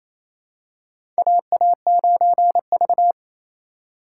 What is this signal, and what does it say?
Morse code, heard as a keyed tone, about 28 wpm, 700 Hz. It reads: AA9V
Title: USA Call Signs - 4 Characters - 2x Character Spacing - 28wpm